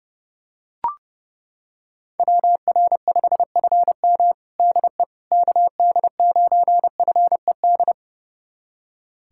Morse code, heard as a keyed tone, about 30 wpm, 700 Hz. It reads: WR5FM DE KD9FED
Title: USA Callsign DE Callsign 30wpm